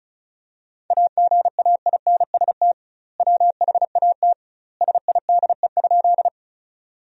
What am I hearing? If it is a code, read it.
Morse code, heard as a keyed tone, about 36 wpm, 700 Hz. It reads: AGAINST WHAT SIDE?
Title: 2x Word-Spacing - Sentences from Top 400 Words - 35wpm